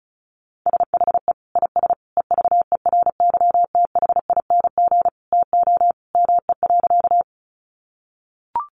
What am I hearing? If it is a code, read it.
Morse code, heard as a keyed tone, about 34 wpm, 700 Hz. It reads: SHE IS EVERYTHING TO ME.
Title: Sentences from Top 700 Words 35wpm